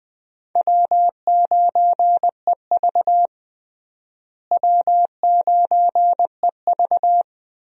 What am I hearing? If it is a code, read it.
Morse code, heard as a keyed tone, about 21 wpm, 700 Hz. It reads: W9EV W9EV
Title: USA Call Signs - 4 Characters - Repeated 2x - 20wpm